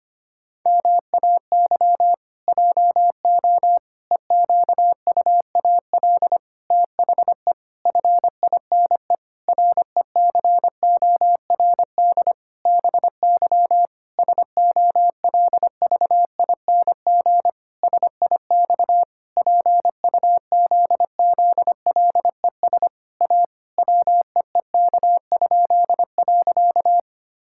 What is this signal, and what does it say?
Morse code, heard as a keyed tone, about 25 wpm, 700 Hz. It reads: MAY JO EQUAL THE FINE RECORD BY SOLVING SIX PUZZLES A WEEK?.
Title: Pangram Challenge v2 - 25wpm